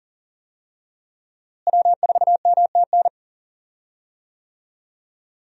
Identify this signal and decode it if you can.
Morse code, heard as a keyed tone, about 40 wpm, 700 Hz. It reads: W4MTN
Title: USA Call Signs - 5 Characters - Repeated 2x - 40wpm